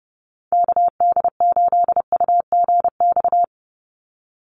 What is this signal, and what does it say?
Morse code, heard as a keyed tone, about 30 wpm, 700 Hz. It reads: KD8UGX
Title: USA Call Signs 30wpm